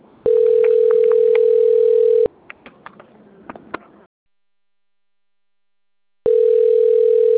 Ringback tone (North America), 440 + 480 Hz.